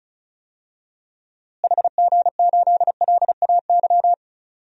Morse code, heard as a keyed tone, about 34 wpm, 700 Hz. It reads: HG8LAY
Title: DXCC-100 Call Signs - 6 Characters - Repeated 2x - 35wpm